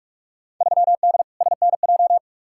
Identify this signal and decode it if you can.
Morse code, heard as a keyed tone, about 46 wpm, 700 Hz. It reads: 3D SNJ
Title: Field Day Exchanges Encore - 45wpm